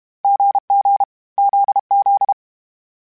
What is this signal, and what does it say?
Morse code, heard as a keyed tone, about 32 wpm, 800 Hz. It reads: GG ZZ
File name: Short Runs / Short Runs-mp3s / GZ GZ7 @30